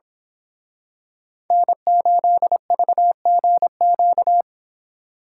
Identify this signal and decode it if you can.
Morse code, heard as a keyed tone, about 27 wpm, 700 Hz. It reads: N8VGQ
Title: Speed Racing - USA Call Signs 22wpm